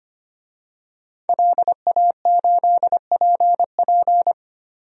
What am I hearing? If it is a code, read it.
Morse code, heard as a keyed tone, about 25 wpm, 700 Hz. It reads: LA8PP
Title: DXCC-100 Call Signs - 5 Characters - 25wpm